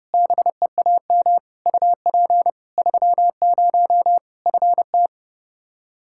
Morse code, heard as a keyed tone, about 30 wpm, 700 Hz. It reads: BEAM UP 30 FT
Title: Practice QSO - Exchange 2 - Info - 30wpm